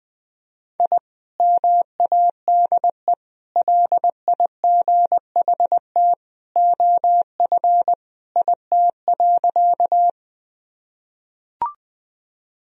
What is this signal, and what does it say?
Morse code, heard as a keyed tone, about 20 wpm, 700 Hz. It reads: I MADE LIGHT OF IT.
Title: Sentences from Top 1000 Words - Review Part 1 - 20wpm